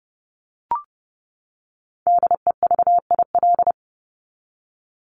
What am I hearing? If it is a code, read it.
Morse code, heard as a keyed tone, about 30 wpm, 700 Hz. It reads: DEVIL
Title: Top 1751-2000 Words 30wpm